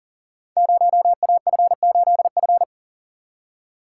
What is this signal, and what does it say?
Morse code, heard as a keyed tone, about 38 wpm, 700 Hz. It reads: 0AF8F
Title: Code Groups - 5 Letter:Numbers - 40wpm